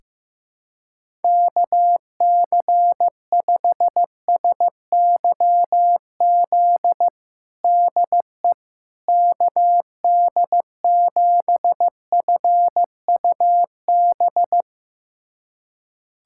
Morse code, heard as a keyed tone, about 15 wpm, 700 Hz. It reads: KC5SYZ DE KD7FUB
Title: USA Callsign DE Callsign - 6 Characters - 15wpm